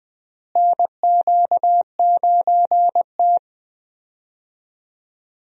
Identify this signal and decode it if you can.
Morse code, heard as a keyed tone, about 20 wpm, 700 Hz. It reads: NQ9T
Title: USA Call Signs - 4 Characters - Repeated 2x - 20wpm